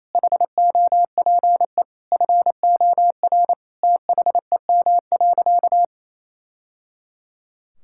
Morse code, heard as a keyed tone, about 28 wpm, 700 Hz. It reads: HOPE FOR THEM.
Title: Sentences from Top 1000 Words - Review Part 5 - 28wpm